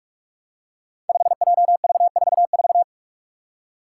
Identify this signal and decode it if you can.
Morse code, heard as a keyed tone, about 44 wpm, 700 Hz. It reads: 5JV44
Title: Five Challenging Letter-Number Combinations 45wpm